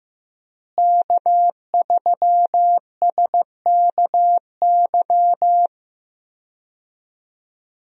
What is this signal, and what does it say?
Morse code, heard as a keyed tone, about 15 wpm, 700 Hz. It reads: K3SKY
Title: USA Call Signs - 5 Characters - Repeated 2x - 15wpm